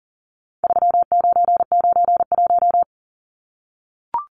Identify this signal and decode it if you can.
Morse code, heard as a keyed tone, about 40 wpm, 700 Hz. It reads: 3991
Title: Numbers - 4-Digit 40wpm